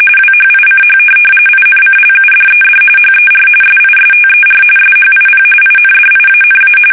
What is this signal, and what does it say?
RTTY, about 45.45 baud. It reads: WHICH INCLUDED HER SISTER SHIP INDIANA
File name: fsk850 / USS-Massachusetts